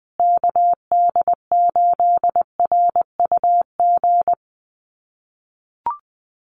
Morse code, heard as a keyed tone, about 20 wpm, 700 Hz. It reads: KD8RUG
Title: USA Call Signs Repeated 2x 20wpm